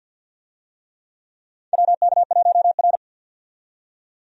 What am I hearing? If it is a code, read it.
Morse code, heard as a keyed tone, about 52 wpm, 700 Hz. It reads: WK1R
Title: USA Call Signs - 4 Characters - 50wpm